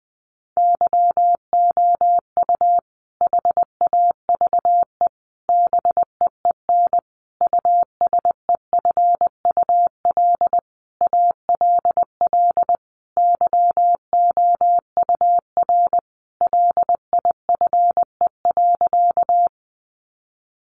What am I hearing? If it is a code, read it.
Morse code, heard as a keyed tone, about 20 wpm, 700 Hz. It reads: YOU HAVE BEEN USEFUL ALL YOUR LIFE.
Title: Sentences from Top 700 Words 20wpm